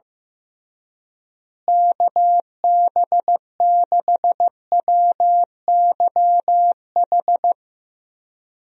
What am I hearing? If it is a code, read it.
Morse code, heard as a keyed tone, about 15 wpm, 700 Hz. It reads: KB6WYH
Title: USA Call Signs - 6 Characters - Repeated 3x - 15wpm